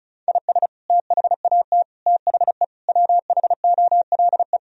Morse code, heard as a keyed tone, about 34 wpm, 700 Hz. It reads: IS THAT THE WHOLE
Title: Sentences from Top 1000 Words - Review Part 4 - 35wpm